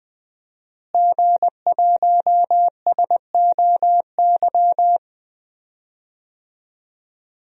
Morse code, heard as a keyed tone, about 21 wpm, 700 Hz. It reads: G1SOY
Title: DXCC-100 Call Signs - 5 Characters - Repeated 2x - 20wpm